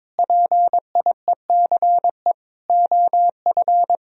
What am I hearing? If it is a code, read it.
Morse code, heard as a keyed tone, about 22 wpm, 700 Hz. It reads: PIECE OF
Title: Sentences from Top 1000 Words - Review Part 3 - 22wpm